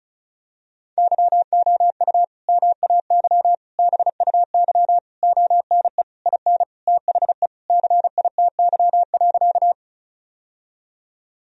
Morse code, heard as a keyed tone, about 36 wpm, 700 Hz. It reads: YOU MAY BUY ONE IN THE CITY.
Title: Sentences from Top 1000 Words - Review Part 4 - 35wpm